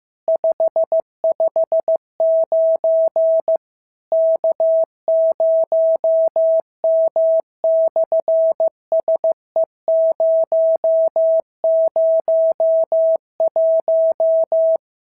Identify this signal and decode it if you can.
Morse code, heard as a keyed tone, about 15 wpm, 650 Hz. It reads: 559 K0M/SE001
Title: SOTA QSO Practice 15wpm